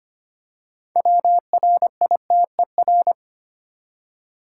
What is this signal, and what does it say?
Morse code, heard as a keyed tone, about 25 wpm, 700 Hz. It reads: WRITER
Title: Six Letter Words - 25wpm